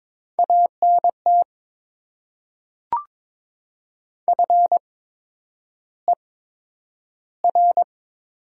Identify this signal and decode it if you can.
Morse code, heard as a keyed tone, about 22 wpm, spaced as for 9 wpm, 700 Hz. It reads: ANT FER
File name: Top 20 QSO Elements - 8x Character Spacing - 22wpm